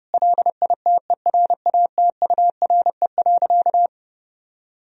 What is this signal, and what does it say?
Morse code, heard as a keyed tone, about 30 wpm, 700 Hz. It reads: LITERATURE.
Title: Sentences from Top 1000 Words - Review Part 3 - 30wpm